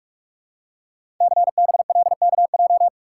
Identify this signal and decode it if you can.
Morse code, heard as a keyed tone, about 44 wpm, 700 Hz. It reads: CBLKJ
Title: Code Groups - 5 Letters - 45wpm